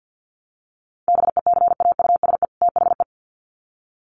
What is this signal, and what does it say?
Morse code, heard as a keyed tone, about 50 wpm, 700 Hz. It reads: BECAUSE THE
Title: Sets of 2 Words 50wpm